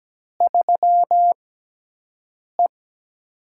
Morse code, heard as a keyed tone, about 17 wpm, spaced as for 6 wpm, 700 Hz. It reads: 3E
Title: USA Call Signs - 4 Characters - 6x Character Spacing - 17wpm